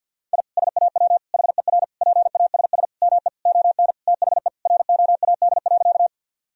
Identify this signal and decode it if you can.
Morse code, heard as a keyed tone, about 50 wpm, 700 Hz. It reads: I SAW HER PASS ME ON THE ROAD.
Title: Sentences from Top 1000 Words - Review Part 2 - 50wpm